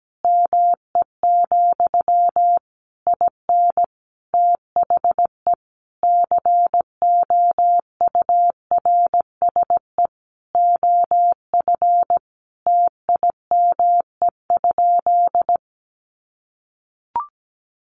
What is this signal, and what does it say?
Morse code, heard as a keyed tone, about 17 wpm, 700 Hz. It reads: ME, IN THE COURSE OF TIME?
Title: Sentences from Top 1200 Words 17wpm